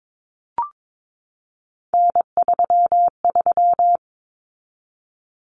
Morse code, heard as a keyed tone, about 22 wpm, 700 Hz. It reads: N33
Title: Common Cut Numbers - 3-Digit 22wpm